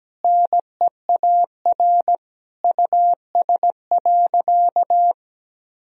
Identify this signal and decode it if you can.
Morse code, heard as a keyed tone, about 17 wpm, 700 Hz. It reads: NEAR US.
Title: Sentences from Top 1000 Words - Review Part 1 - 17wpm